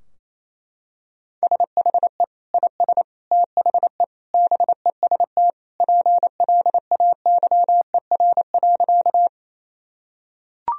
Morse code, heard as a keyed tone, about 27 wpm, 700 Hz. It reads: SHE IS THE BEST PLAYER.
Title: Sentences from Top 1000 Words - Review Part 4 - 28wpm